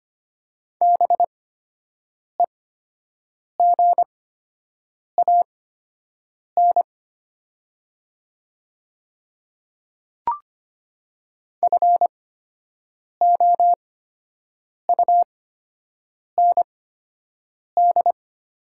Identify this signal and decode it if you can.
Morse code, heard as a keyed tone, about 25 wpm, spaced as for 7 wpm, 700 Hz. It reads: BEGAN FOUND
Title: 8x Character Spacing - Top 500 Words - 25wpm